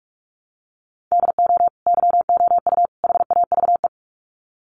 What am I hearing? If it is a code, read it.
Morse code, heard as a keyed tone, about 44 wpm, 700 Hz. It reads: DO YOU HAVE
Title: Sets of 3 Words 45wpm